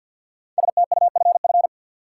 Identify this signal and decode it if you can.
Morse code, heard as a keyed tone, about 50 wpm, 700 Hz. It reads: STUFF
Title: Sentences from Top 1400 Words - New Word Drill - 50wpm